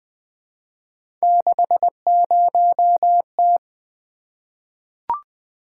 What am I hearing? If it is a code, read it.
Morse code, heard as a keyed tone, about 20 wpm, 700 Hz. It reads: 60T
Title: Three Letter-Number Combinations - 20wpm